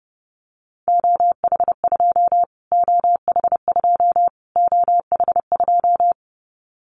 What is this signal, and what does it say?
Morse code, heard as a keyed tone, about 30 wpm, 700 Hz. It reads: OH2 OH2 OH2
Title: Callsign Prefixes for Top 25 DXCC - Repeated 3x - 30wpm